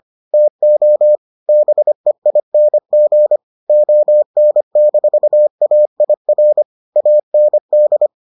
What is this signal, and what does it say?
Morse code, heard as a keyed tone, about 25 wpm, 600 Hz. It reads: TO BEING ON-AIR AND